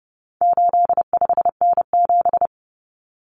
Morse code, heard as a keyed tone, about 30 wpm, 700 Hz. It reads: 85N7
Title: Common Cut Numbers - 4-Digit 30wpm